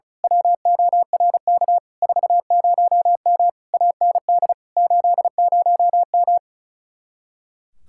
Morse code, heard as a keyed tone, about 33 wpm, 700 Hz. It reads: WORK 40M AND 80M
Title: Speed-Racing Common QSO Phrases - 35wpm